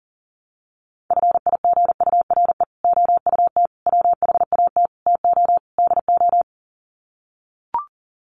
Morse code, heard as a keyed tone, about 37 wpm, 700 Hz. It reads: FIGURE OUT WHAT TO DO
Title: Sets of 5 Words Encore2 40wpm